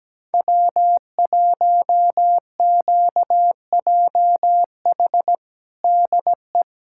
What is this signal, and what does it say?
Morse code, heard as a keyed tone, about 17 wpm, 700 Hz. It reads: W1QJH DE
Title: USA Callsign DE Callsign 17wpm